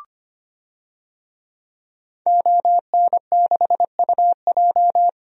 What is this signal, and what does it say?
Morse code, heard as a keyed tone, about 25 wpm, 700 Hz. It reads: ON6UJ
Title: DXCC-100 Call Signs - 5 Characters - Repeated 3x - 25wpm